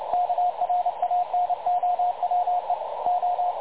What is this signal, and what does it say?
Morse code, heard as a keyed tone, about 29 wpm, 720 Hz, with radio noise. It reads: KRANOWE O
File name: dlazaawansowanych / nagraniazpasma / websdr recording 3565.0kHz